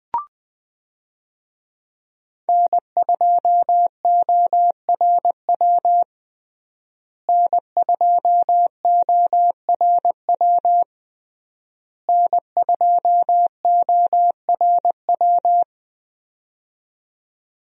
Morse code, heard as a keyed tone, about 21 wpm, 700 Hz. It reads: N2ORW N2ORW N2ORW
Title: USA Call Signs - 5 Characters - Repeated 3x - 20wpm